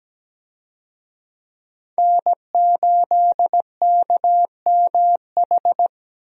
Morse code, heard as a keyed tone, about 17 wpm, 700 Hz. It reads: N8KMH
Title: USA Call Signs - 5 Characters - Repeated 3x - 17wpm